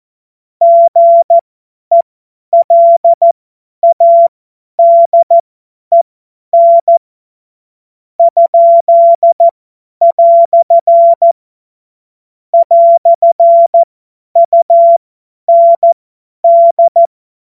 Morse code, heard as a keyed tone, about 14 wpm, spaced as for 10 wpm, 680 Hz. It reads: GELADEN ?" "UND
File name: Witz-02-September-2025-50-F70